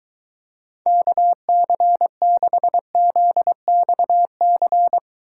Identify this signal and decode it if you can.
Morse code, heard as a keyed tone, about 22 wpm, 700 Hz. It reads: KC6ZXC